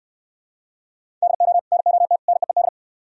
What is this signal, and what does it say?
Morse code, heard as a keyed tone, about 68 wpm, 700 Hz. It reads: DO NOT NEED